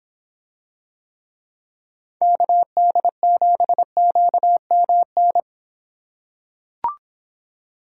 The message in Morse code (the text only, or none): KD7QMN